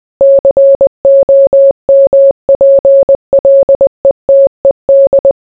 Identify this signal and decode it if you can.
Morse code, heard as a keyed tone, about 20 wpm, 550 Hz. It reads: COMPLETED